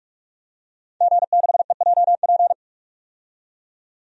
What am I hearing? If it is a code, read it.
Morse code, heard as a keyed tone, about 42 wpm, 700 Hz. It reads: G6EJP
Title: DXCC-100 Call Signs - 5 Characters - 45wpm